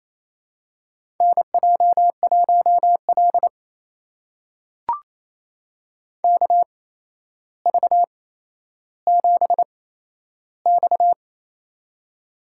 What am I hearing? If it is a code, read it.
Morse code, heard as a keyed tone, about 27 wpm, spaced as for 12 wpm, 700 Hz. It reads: NJ1L KV7X
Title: USA Call Signs - 4 Characters - 8x Character Spacing - 28wpm